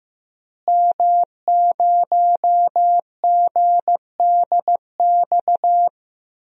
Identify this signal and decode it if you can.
Morse code, heard as a keyed tone, about 15 wpm, 700 Hz. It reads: M0GDX
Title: DXCC-100 Call Signs - 5 Characters - Repeated 2x - 15wpm